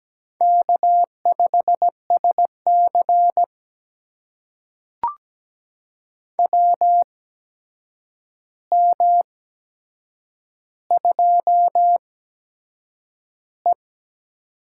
Morse code, heard as a keyed tone, about 17 wpm, spaced as for 7 wpm, 700 Hz. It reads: K5SC WM2E
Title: USA Call Signs - 4 Characters - 8x Character Spacing - 17wpm